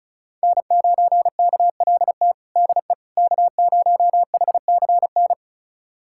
Morse code, heard as a keyed tone, about 34 wpm, 700 Hz. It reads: N9KLT DE K0HCN
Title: USA Callsign DE Callsign - 5 Characters - 35wpm